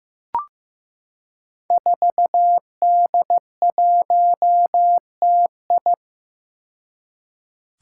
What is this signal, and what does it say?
Morse code, heard as a keyed tone, about 15 wpm, 700 Hz. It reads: 4D1TI
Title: Five Challenging Letter-Number Combinations 15wpm